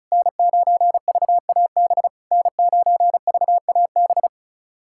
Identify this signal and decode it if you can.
Morse code, heard as a keyed tone, about 36 wpm, 690 Hz. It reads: N9VAB N9VAB
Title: SOTA QSO Practice 35wpm